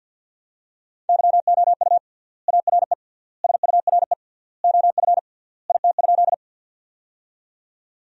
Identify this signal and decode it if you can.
Morse code, heard as a keyed tone, about 52 wpm, 700 Hz. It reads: YOU ARE SURE OF IT?
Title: 3x Word-Spacing - Sentences from Top 300 Words - 50wpm